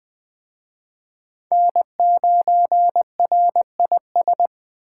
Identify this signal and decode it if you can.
Morse code, heard as a keyed tone, about 24 wpm, 700 Hz. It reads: N9RIS
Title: Speed Racing - USA Call Signs 20wpm